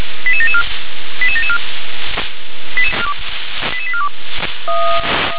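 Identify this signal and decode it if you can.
Five-tone selective call (ZVEI), heard as quick runs of five tones, tones of about 70 ms. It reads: 88774 88774 88842
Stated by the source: ZVEI five-tone call received on SDR.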